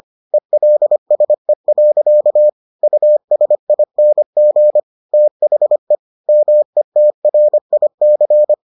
Morse code, heard as a keyed tone, about 25 wpm, 600 Hz. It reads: ELSE. USING THE METRIC